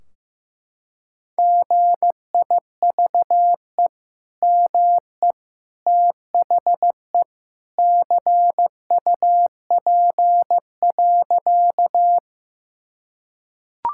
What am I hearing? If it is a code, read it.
Morse code, heard as a keyed tone, about 16 wpm, 700 Hz. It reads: GIVE ME THE CUP.
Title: Sentences from Top 1600 Words 15wpm